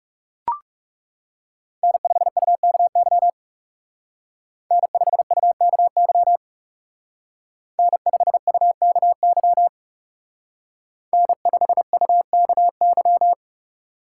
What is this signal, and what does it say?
Morse code, heard as a keyed tone, about 36 wpm, 700 Hz. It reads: N5UKY N5UKY N5UKY N5UKY
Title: Speed Racing - USA 1x3 Call Signs 30wpm